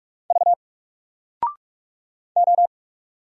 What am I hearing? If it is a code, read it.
Morse code, heard as a keyed tone, about 44 wpm, 700 Hz. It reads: V O
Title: Single Character (Letter:Number:Punctuation:Prosign) 45wpm